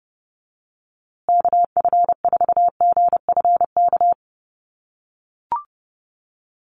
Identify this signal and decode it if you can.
Morse code, heard as a keyed tone, about 30 wpm, 700 Hz. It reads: KF4GFK
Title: USA Call Signs Repeated 2x 30wpm